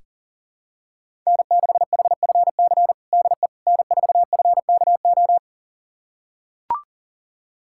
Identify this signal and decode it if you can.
Morse code, heard as a keyed tone, about 40 wpm, 700 Hz. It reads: N6HFC DE N4FKO
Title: USA Callsign DE Callsign - 5 Characters - 40wpm